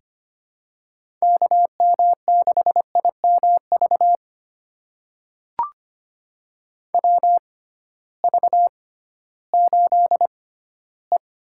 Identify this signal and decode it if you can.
Morse code, heard as a keyed tone, about 25 wpm, spaced as for 15 wpm, 700 Hz. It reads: KM6IMV WV8E
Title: USA Call Signs - 6 Characters - 6x Character Spacing - 25wpm